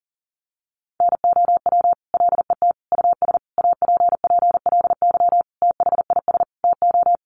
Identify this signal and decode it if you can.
Morse code, heard as a keyed tone, about 40 wpm, 700 Hz. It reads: NOW LET US APPLY THIS TO
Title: Sentences from Top 1200 Words 40wpm